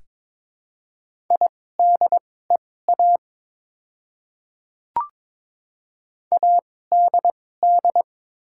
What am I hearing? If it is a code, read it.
Morse code, heard as a keyed tone, about 22 wpm, spaced as for 16 wpm, 700 Hz. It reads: IDEA ADD
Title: 2x Character Spacing - Top 300 Words - 22wpm